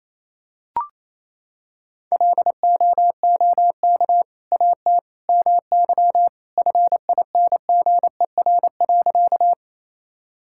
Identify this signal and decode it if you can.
Morse code, heard as a keyed tone, about 28 wpm, 700 Hz. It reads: LOOK AT MY FINGER.
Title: Sentences from Top 1600 Words 28wpm